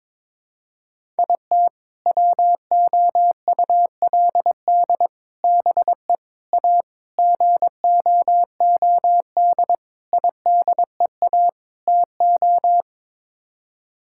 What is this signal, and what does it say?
Morse code, heard as a keyed tone, about 22 wpm, 700 Hz. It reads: IT WOULD BE A GOOD IDEA TO
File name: practice / 22wpm / Sets of 7 Words - 22wpm